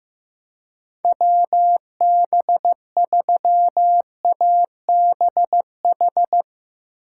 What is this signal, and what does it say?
Morse code, heard as a keyed tone, about 15 wpm, 700 Hz. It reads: WB3ABH
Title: USA Call Signs - 6 Characters - 2x Character Spacing - 15wpm